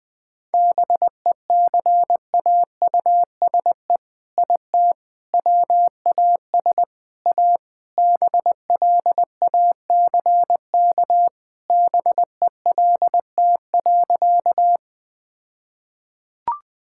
Morse code, heard as a keyed tone, about 20 wpm, 700 Hz. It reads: BECAUSE IT WAS A BLACK BELT.
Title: All-Age Appropriate Jokes - 20wpm